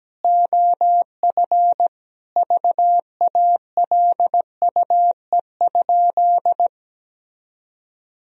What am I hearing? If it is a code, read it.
Morse code, heard as a keyed tone, about 17 wpm, 700 Hz. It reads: OF VALUE?
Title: Sentences from Top 1000 Words - Review Part 3 - 17wpm